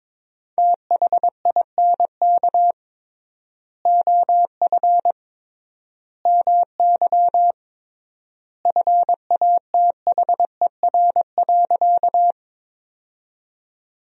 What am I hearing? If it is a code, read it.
Morse code, heard as a keyed tone, about 22 wpm, 700 Hz. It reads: THINK OF MY FATHER.
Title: 3x Word-Spacing - Sentences from Top 300 Words - 22wpm